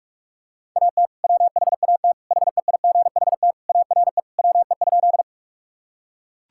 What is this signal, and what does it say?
Morse code, heard as a keyed tone, about 43 wpm, 700 Hz. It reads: AT WHAT HEIGHT ARE WE?
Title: Sentences from Top 1600 Words 45wpm